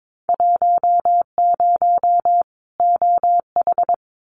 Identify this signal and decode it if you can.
Morse code, heard as a keyed tone, about 22 wpm, 700 Hz. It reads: 10 OH
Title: Winter Field Day Exchanges 22wpm